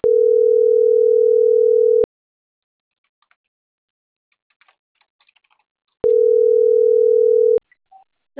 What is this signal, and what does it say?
Ringback tone (North America), 440 + 480 Hz.